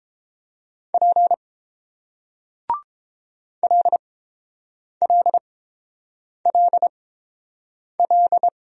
Morse code, heard as a keyed tone, about 29 wpm, spaced as for 7 wpm, 700 Hz. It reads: P LLLL